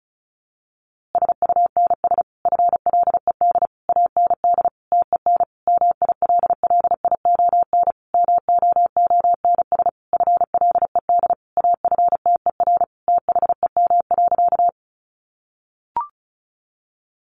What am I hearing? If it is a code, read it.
Morse code, heard as a keyed tone, about 34 wpm, 700 Hz. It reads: SUNS FLED AND TEN MILLION MOONS FLED AFTER THEM.
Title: A Sound Of Thunder 35wpm